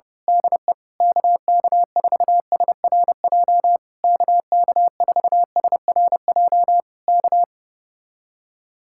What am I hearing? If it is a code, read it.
Morse code, heard as a keyed tone, about 31 wpm, 700 Hz. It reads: DE KK4SRJ KK4SRJ K
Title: Speed-Racing Common QSO Phrases - 25wpm